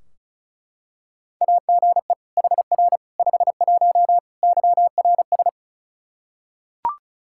Morse code, heard as a keyed tone, about 34 wpm, 700 Hz. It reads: AGE HR 51 YRS
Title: Practice QSO - Exchange 3 - More Info - 35wpm